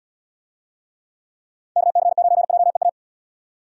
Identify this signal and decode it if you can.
Morse code, heard as a keyed tone, about 78 wpm, 700 Hz. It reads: KC9JEA